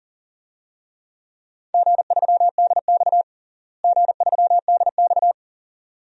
Morse code, heard as a keyed tone, about 40 wpm, 700 Hz. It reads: G3DX G3DX
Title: DXCC-100 Call Signs - 4 Characters - Repeated 2x - 40wpm